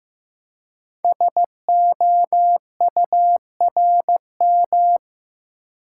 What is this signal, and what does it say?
Morse code, heard as a keyed tone, about 15 wpm, 700 Hz. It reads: SOURM
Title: Code Groups - 5 Letters - 15wpm